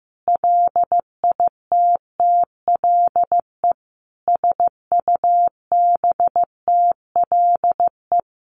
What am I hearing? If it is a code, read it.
Morse code, heard as a keyed tone, about 15 wpm, 700 Hz. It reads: LITTLE SUBTLE
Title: A Sound Of Thunder 15wpm